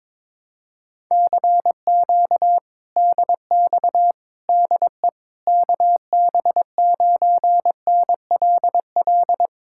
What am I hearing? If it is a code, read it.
Morse code, heard as a keyed tone, about 22 wpm, 700 Hz. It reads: CQ DX DE KB9NLL
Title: Speed-Racing Common QSO Phrases - 22wpm